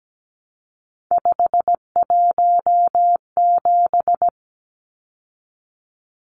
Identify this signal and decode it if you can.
Morse code, heard as a keyed tone, about 17 wpm, 700 Hz. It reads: 517
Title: Numbers - 3-Digit 17wpm